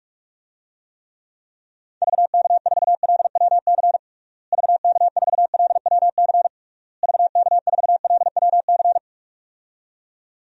Morse code, heard as a keyed tone, about 46 wpm, 700 Hz. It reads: VK4LWC VK4LWC VK4LWC
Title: DXCC-100 Call Signs - 6 Characters - Repeated 3x - 45wpm